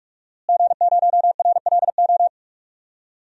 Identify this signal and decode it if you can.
Morse code, heard as a keyed tone, about 46 wpm, 700 Hz. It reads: G0RLO
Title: DXCC-100 Call Signs - 5 Characters - Repeated 3x - 45wpm